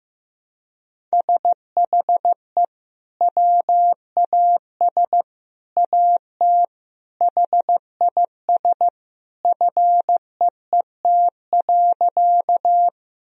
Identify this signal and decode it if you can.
Morse code, heard as a keyed tone, about 16 wpm, 700 Hz. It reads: SHE WAS AT HIS FEET.
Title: Sentences from Top 1000 Words - Review Part 1 - 15wpm